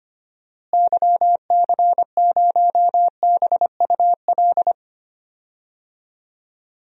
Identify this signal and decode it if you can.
Morse code, heard as a keyed tone, about 25 wpm, 700 Hz. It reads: YC0BUL
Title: DXCC-100 Call Signs - 6 Characters - Repeated 3x - 25wpm